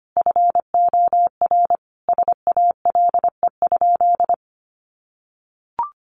Morse code, heard as a keyed tone, about 25 wpm, 700 Hz. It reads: FOR SALE?
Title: Sentences from Top 1400 Words 25wpm